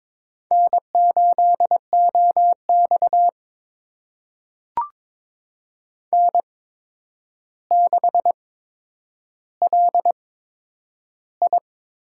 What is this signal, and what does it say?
Morse code, heard as a keyed tone, about 22 wpm, spaced as for 9 wpm, 700 Hz. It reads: N8OX N6LI
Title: USA Call Signs - 4 Characters - 8x Character Spacing - 22wpm